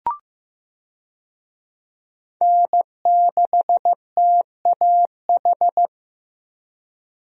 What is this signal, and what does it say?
Morse code, heard as a keyed tone, about 15 wpm, 700 Hz. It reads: N6TAH
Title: USA Call Signs - 5 Characters - Repeated 2x - 15wpm